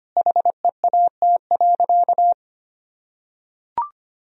Morse code, heard as a keyed tone, about 25 wpm, 700 Hz. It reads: HEAT.
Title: Sentences from Top 1000 Words - Review Part 3 - 25wpm